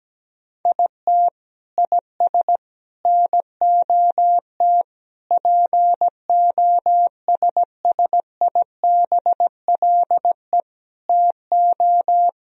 Morse code, heard as a keyed tone, about 17 wpm, 700 Hz. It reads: IT IS NOT POSSIBLE TO
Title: Sentences from Top 1000 Words - Review Part 4 - 17wpm